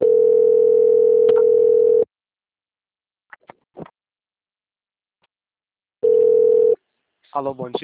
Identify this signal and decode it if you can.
Ringback tone (North America), 440 + 480 Hz.